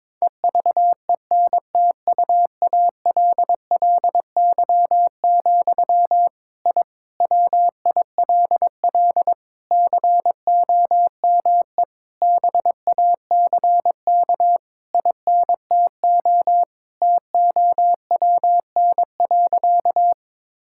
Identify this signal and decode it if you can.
Morse code, heard as a keyed tone, about 22 wpm, 700 Hz. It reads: EVENTUALLY, I WILL COME BACK INTO TOWN.
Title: Sentences from Top 1000 Words - Review Part 6 - 22wpm